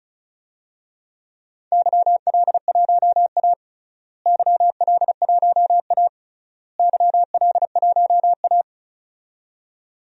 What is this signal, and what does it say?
Morse code, heard as a keyed tone, about 36 wpm, 700 Hz. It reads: YL1A YL1A YL1A
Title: DXCC-100 Call Signs - 4 Characters - Repeated 3x - 35wpm